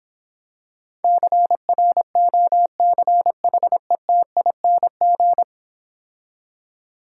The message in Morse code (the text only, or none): CROCHETING